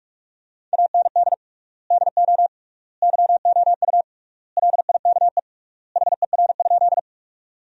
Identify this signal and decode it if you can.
Morse code, heard as a keyed tone, about 46 wpm, 700 Hz. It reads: AND DO YOU LIKE HER?
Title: 3x Word-Spacing - Sentences from Top 100 Words - 45wpm